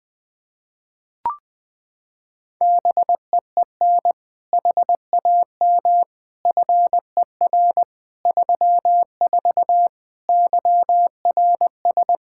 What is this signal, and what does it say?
Morse code, heard as a keyed tone, about 20 wpm, 700 Hz. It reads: BEEN HAM FER 34 YRS
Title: Practice QSO - Exchange 3 - More Info - 20wpm